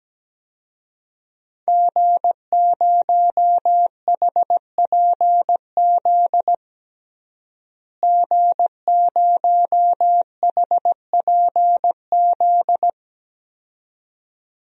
Morse code, heard as a keyed tone, about 17 wpm, 700 Hz. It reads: G0HPZ G0HPZ
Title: DXCC-100 Call Signs - 5 Characters - Repeated 2x - 17wpm